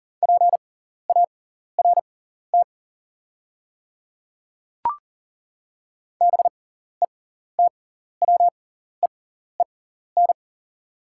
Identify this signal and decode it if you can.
Morse code, heard as a keyed tone, about 38 wpm, spaced as for 14 wpm, 690 Hz. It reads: PART BETWEEN
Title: 6x Character Spacing - Top 300 Words - 40wpm